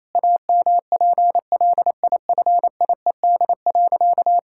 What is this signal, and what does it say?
Morse code, heard as a keyed tone, about 28 wpm, 700 Hz. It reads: AMPLIFIED.
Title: Pangram Challenge v2 - 28wpm